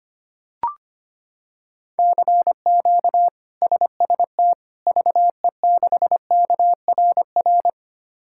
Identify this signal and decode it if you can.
Morse code, heard as a keyed tone, about 25 wpm, 700 Hz. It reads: CQ SST VE6KRR
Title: K1USN SST QSO Practice - 25wpm